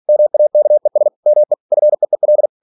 Morse code, heard as a keyed tone, about 46 wpm, 600 Hz. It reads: MAKES ME FEEL
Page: Never 47wpm